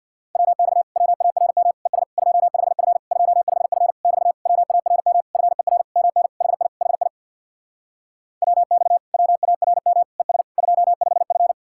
Morse code, heard as a keyed tone, about 66 wpm, 700 Hz. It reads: WX WARM ES 25F 25F <BT> WARM FER MN HI HI WX WARM ES 25F